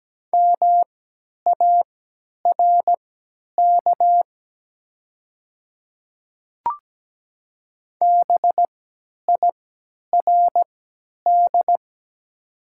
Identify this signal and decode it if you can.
Morse code, heard as a keyed tone, about 17 wpm, spaced as for 10 wpm, 700 Hz. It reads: MARK BIRD
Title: 3x Character Spacing - Top 500 Words - 17wpm